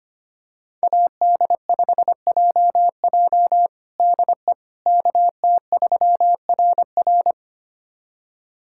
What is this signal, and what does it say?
Morse code, heard as a keyed tone, about 25 wpm, 700 Hz. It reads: AD5JJ DE KT3RR
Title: USA Callsign DE Callsign - 5 Characters - 25wpm